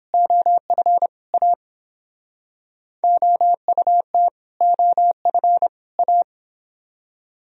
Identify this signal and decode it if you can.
Morse code, heard as a keyed tone, about 31 wpm, 700 Hz. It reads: OF A OUT OF A